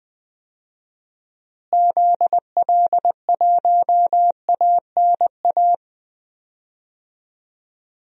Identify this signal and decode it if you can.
Morse code, heard as a keyed tone, about 20 wpm, 700 Hz. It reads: ZL1ANA
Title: DXCC-100 Call Signs - 6 Characters - Repeated 3x - 20wpm